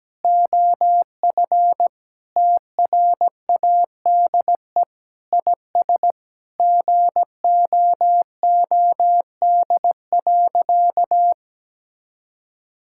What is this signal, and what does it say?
Morse code, heard as a keyed tone, about 17 wpm, 700 Hz. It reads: OF TRADE IS GOOD.
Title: Sentences from Top 1000 Words - Review Part 6 - 17wpm